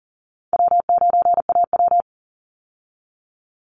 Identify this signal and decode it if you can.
Morse code, heard as a keyed tone, about 39 wpm, 700 Hz. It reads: W9AW
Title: Call Signs Repeated 3x - 40wpm